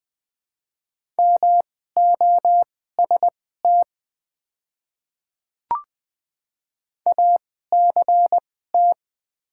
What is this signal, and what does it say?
Morse code, heard as a keyed tone, about 20 wpm, spaced as for 14 wpm, 700 Hz. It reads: MOST ACT
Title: 2x Character Spacing - Top 200 Words - 20wpm